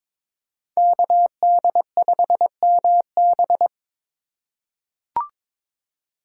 Morse code, heard as a keyed tone, about 22 wpm, 700 Hz. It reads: KD5MB
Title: USA Call Signs - 5 Characters - 2x Character Spacing - 22wpm